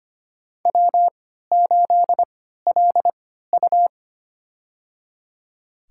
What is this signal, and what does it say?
Morse code, heard as a keyed tone, about 25 wpm, spaced as for 14 wpm, 700 Hz. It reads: W8LU
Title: USA Call Signs - 4 Characters - 3x Character Spacing - 25wpm